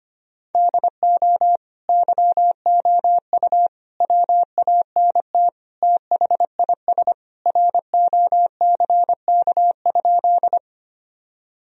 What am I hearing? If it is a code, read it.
Morse code, heard as a keyed tone, about 25 wpm, 700 Hz. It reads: DO YOU WANT THIS ROCK?
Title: Sentences from Top 1000 Words - Review Part 2 - 25wpm